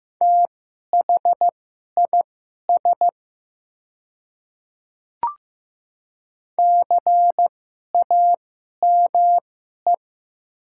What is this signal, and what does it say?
Morse code, heard as a keyed tone, about 15 wpm, spaced as for 11 wpm, 690 Hz. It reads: THIS CAME
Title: 2x Character Spacing - Top 300 Words - 15wpm